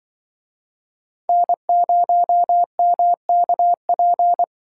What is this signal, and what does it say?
Morse code, heard as a keyed tone, about 26 wpm, 700 Hz. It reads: N0MKP